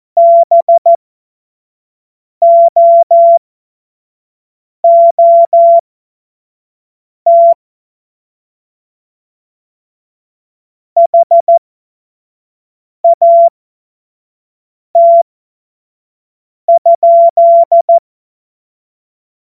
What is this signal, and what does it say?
Morse code, heard as a keyed tone, about 14 wpm, spaced as for 5 wpm, 680 Hz. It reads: BOOT HAT?
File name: Morsekurs / Witze / Witz 02. November 2021, 25 F70